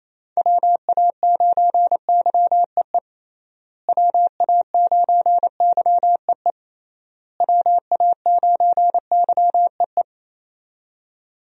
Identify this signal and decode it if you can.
Morse code, heard as a keyed tone, about 28 wpm, 700 Hz. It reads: WA9YEE WA9YEE WA9YEE
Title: USA Call Signs - 6 Characters - Repeated 3x - 28wpm